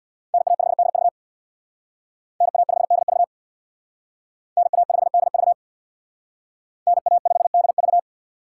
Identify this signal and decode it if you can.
Morse code, heard as a keyed tone, about 64 wpm, 690 Hz. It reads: NA5DV NA5DV NA5DV NA5DV